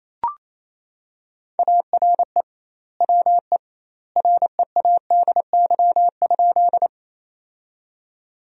Morse code, heard as a keyed tone, about 28 wpm, 700 Hz. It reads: ARE WE READY?
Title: 2x Word-Spacing - Sentences from Top 400 Words - 28wpm